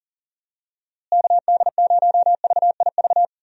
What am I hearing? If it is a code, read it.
Morse code, heard as a keyed tone, about 40 wpm, 700 Hz. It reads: KD0VIV
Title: USA Call Signs - 6 Characters - Repeated 3x - 40wpm